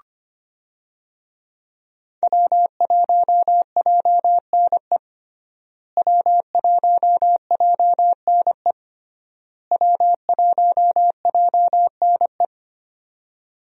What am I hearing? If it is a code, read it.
Morse code, heard as a keyed tone, about 25 wpm, 700 Hz. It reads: W1JNE W1JNE W1JNE
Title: USA Call Signs - 5 Characters - Repeated 3x - 25wpm